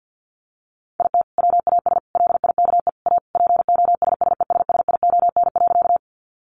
Morse code, heard as a keyed tone, about 50 wpm, 700 Hz. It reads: IT WAS LIKE A POSSESSION.
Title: Sentences from Top 1400 Words 50wpm